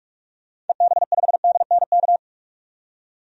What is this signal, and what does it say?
Morse code, heard as a keyed tone, about 41 wpm, 700 Hz. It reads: EB5DNK
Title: DXCC-100 Call Signs - 6 Characters - 45wpm